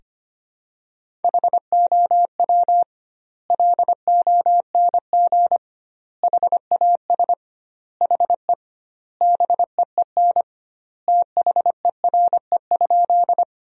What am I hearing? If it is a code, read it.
Morse code, heard as a keyed tone, about 25 wpm, 700 Hz. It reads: HOW LONG HAS HE BEEN THERE?
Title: 2x Word-Spacing - Sentences from Top 100 Words - 25wpm